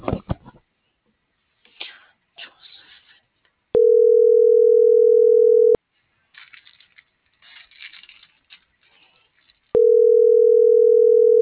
Ringback tone (North America), 440 + 480 Hz.